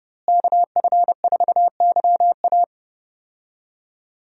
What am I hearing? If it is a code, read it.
Morse code, heard as a keyed tone, about 30 wpm, 700 Hz. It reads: KF4YA
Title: USA Call Signs - 5 Characters - Repeated 3x - 30wpm